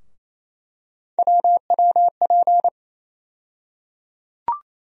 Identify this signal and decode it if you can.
Morse code, heard as a keyed tone, about 28 wpm, 700 Hz. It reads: WWP
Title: USA Call Sign Postfixes - 3 Letters 28wpm